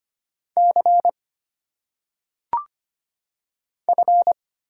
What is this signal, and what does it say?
Morse code, heard as a keyed tone, about 25 wpm, 700 Hz. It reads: C F
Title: Single Letters - Phonetic Pronunciation 25wpm